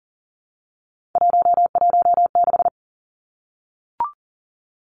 Morse code, heard as a keyed tone, about 40 wpm, 700 Hz. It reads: 116
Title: Three Challenging Letter-Number Combinations 40wpm